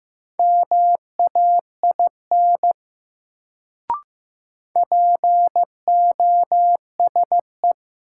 Morse code, heard as a keyed tone, about 15 wpm, 700 Hz. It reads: MAIN POSE
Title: Sentences from Top 1000 Words - Review Part 2 - New Word Drill - 15wpm